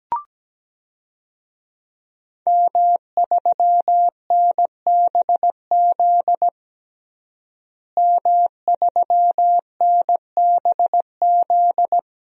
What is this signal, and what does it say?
Morse code, heard as a keyed tone, about 17 wpm, 700 Hz. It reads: M3NBZ M3NBZ
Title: DXCC-100 Call Signs - 5 Characters - Repeated 2x - 17wpm